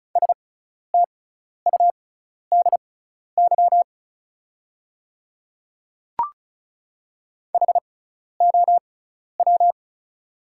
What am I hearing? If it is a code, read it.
Morse code, heard as a keyed tone, about 36 wpm, spaced as for 12 wpm, 690 Hz. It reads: STUDY HOW
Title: 6x Character Spacing - Top 300 Words - 35wpm